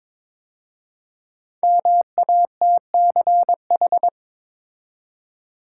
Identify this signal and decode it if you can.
Morse code, heard as a keyed tone, about 23 wpm, 700 Hz. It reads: MATCH
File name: 5 Letter Words - First 3 Letters Given - 22wpm